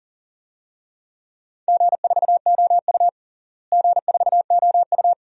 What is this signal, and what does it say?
Morse code, heard as a keyed tone, about 40 wpm, 700 Hz. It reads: G4OU G4OU
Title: DXCC-100 Call Signs - 4 Characters - Repeated 3x - 40wpm